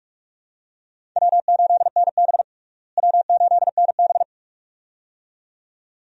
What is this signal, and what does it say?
Morse code, heard as a keyed tone, about 43 wpm, 700 Hz. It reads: W8NB W8NB
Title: USA Call Signs - 4 Characters - Repeated 2x - 45wpm